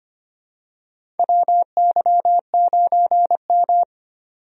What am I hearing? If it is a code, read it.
Morse code, heard as a keyed tone, about 25 wpm, 700 Hz. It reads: WY9M
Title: USA Call Signs - 4 Characters - 25wpm